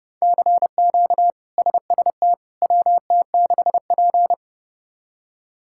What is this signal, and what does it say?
Morse code, heard as a keyed tone, about 30 wpm, 700 Hz. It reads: CQ SST WT6P
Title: K1USN SST QSO Practice - 30wpm